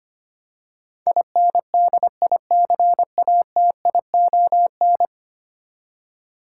Morse code, heard as a keyed tone, about 25 wpm, 700 Hz. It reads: INDICATION
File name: 10 Letter Words - First 3 Letters Given - 25wpm